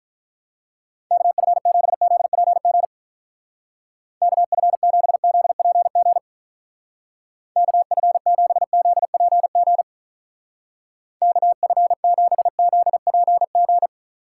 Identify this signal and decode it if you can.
Morse code, heard as a keyed tone, about 42 wpm, 700 Hz. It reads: KF7ZPG KF7ZPG KF7ZPG KF7ZPG
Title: Speed Racing - USA 2x3 Call Signs 35wpm